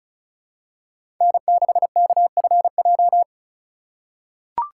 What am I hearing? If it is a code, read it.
Morse code, heard as a keyed tone, about 34 wpm, 700 Hz. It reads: N6KFJ
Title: USA Call Signs - 5 Characters - 2x Character Spacing - 35wpm